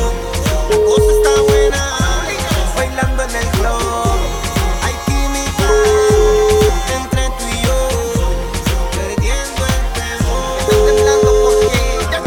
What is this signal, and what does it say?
Telephone call progress tone (Europe, 425 Hz), heard as a telephone network's tone, about 420 Hz, with noise. Ringback tone.